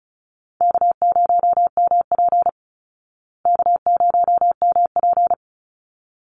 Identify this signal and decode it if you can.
Morse code, heard as a keyed tone, about 34 wpm, 700 Hz. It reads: K0MP K0MP
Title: Call Signs Repeated 3x - 35wpm